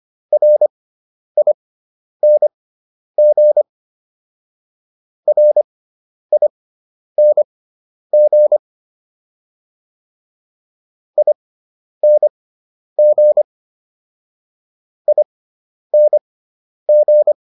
Morse code, heard as a keyed tone, about 25 wpm, spaced as for 10 wpm, 600 Hz. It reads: RING RING ING ING